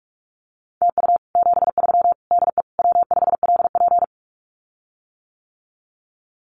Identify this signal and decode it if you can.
Morse code, heard as a keyed tone, about 44 wpm, 700 Hz. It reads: TU 73 DE W5LP
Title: Common QSO Phrases Encore 45wpm